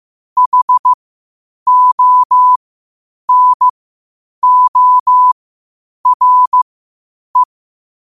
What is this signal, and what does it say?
Morse code, heard as a keyed tone, about 15 wpm, spaced as for 8 wpm, 1000 Hz. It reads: HONORE
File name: qotdfiles / 8wpm